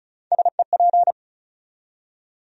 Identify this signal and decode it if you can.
Morse code, heard as a keyed tone, about 34 wpm, 700 Hz. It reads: SEP
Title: USA Call Sign Postfixes - 3 Letters 35wpm